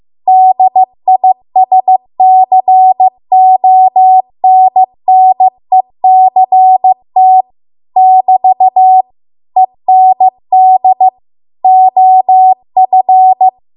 Morse code, heard as a keyed tone, about 15 wpm, 750 Hz. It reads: DISCONNECT <BT> END OF